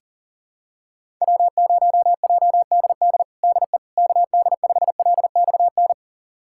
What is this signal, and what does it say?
Morse code, heard as a keyed tone, about 40 wpm, 700 Hz. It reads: W0JDD DE KD5LXN
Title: USA Callsign DE Callsign 40wpm